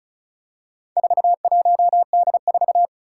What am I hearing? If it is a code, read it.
Morse code, heard as a keyed tone, about 36 wpm, 700 Hz. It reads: 41D4
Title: Four Challenging Letter-Number Combinations 35wpm